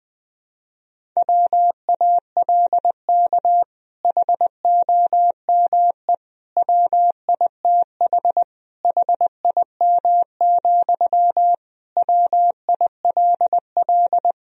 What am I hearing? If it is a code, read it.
Morse code, heard as a keyed tone, about 20 wpm, 700 Hz. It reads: WALK HOME WITH HIM, WILL
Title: Sentences from Top 1000 Words - Review Part 1 - 20wpm